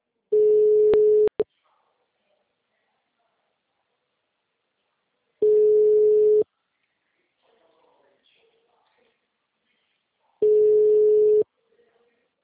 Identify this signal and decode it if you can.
Ringback tone (Europe, 425 Hz).
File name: out- -2035- - - .16